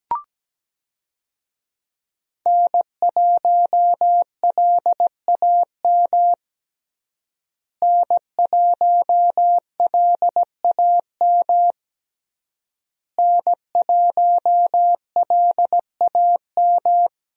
Morse code, heard as a keyed tone, about 17 wpm, 700 Hz. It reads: N1LAM N1LAM N1LAM
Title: USA Call Signs - 5 Characters - Repeated 3x - 17wpm